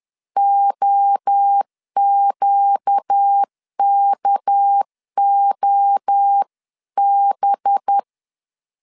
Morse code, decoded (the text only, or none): OQKOB